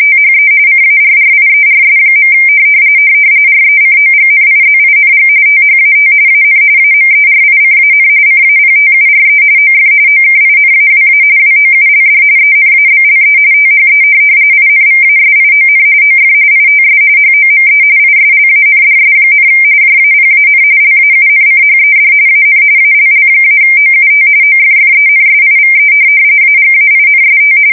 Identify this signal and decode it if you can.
RTTY, about 45.45 baud. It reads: THE CARRIERS. RECOGNIZING THAT AN AMERICAN CAPTURE OF OKINAWA WOULD ENABLE THE INVASION OF ITS HOME ISLANDS, JAPAN LAUNCHED AN ALL-OUT OFFENSIVE AGAINST
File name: fsk170 / USS-New-Jersey